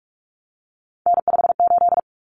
Morse code, heard as a keyed tone, about 44 wpm, 700 Hz. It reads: N58
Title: Common Cut Numbers - 3-Digit 45wpm